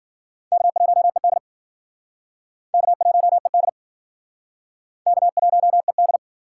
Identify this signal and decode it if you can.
Morse code, heard as a keyed tone, about 58 wpm, 700 Hz. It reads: K1ED K1ED K1ED